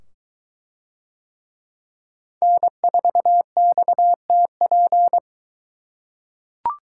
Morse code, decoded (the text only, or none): N4XTP